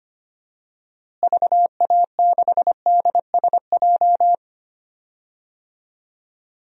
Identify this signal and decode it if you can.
Morse code, heard as a keyed tone, about 25 wpm, 700 Hz. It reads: VA6DSJ
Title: DXCC-100 Call Signs - 6 Characters - Repeated 2x - 25wpm